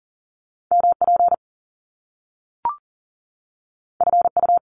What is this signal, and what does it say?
Morse code, heard as a keyed tone, about 38 wpm, 700 Hz. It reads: MP FU
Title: Top Two Letter Combinations in Words 40wpm